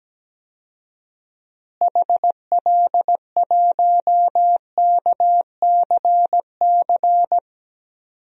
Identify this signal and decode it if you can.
Morse code, heard as a keyed tone, about 17 wpm, 700 Hz. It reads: HL1KCC
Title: DXCC-100 Call Signs - 6 Characters - Repeated 3x - 17wpm